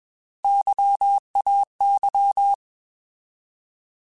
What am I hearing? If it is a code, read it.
Morse code, heard as a keyed tone, about 21 wpm, 800 Hz. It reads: YAY
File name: Alphabetical / Alphabetical mp3s / Ys @20